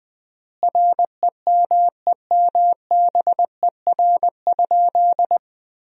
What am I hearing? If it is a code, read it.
Morse code, heard as a keyed tone, about 20 wpm, 700 Hz. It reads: REMEMBER?
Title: 3x Word-Spacing - Sentences from Top 400 Words - 20wpm